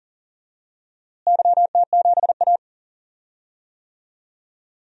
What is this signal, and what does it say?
Morse code, heard as a keyed tone, about 40 wpm, 700 Hz. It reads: YT7A
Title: DXCC-100 Call Signs - 4 Characters - Repeated 3x - 40wpm